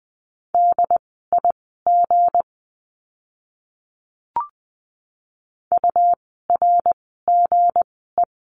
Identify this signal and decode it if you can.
Morse code, heard as a keyed tone, about 20 wpm, spaced as for 14 wpm, 700 Hz. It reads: DIG URGE
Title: 160 Lesson 26f - Letter g - Words 20wpm